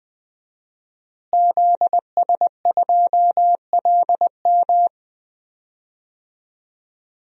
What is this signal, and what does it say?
Morse code, heard as a keyed tone, about 20 wpm, 700 Hz. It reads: ZS2LM
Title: DXCC-100 Call Signs - 5 Characters - Repeated 3x - 20wpm